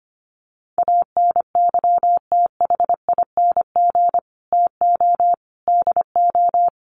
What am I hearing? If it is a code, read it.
Morse code, heard as a keyed tone, about 25 wpm, 700 Hz. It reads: ANYTHING TO DO
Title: Sets of 4 Words 25wpm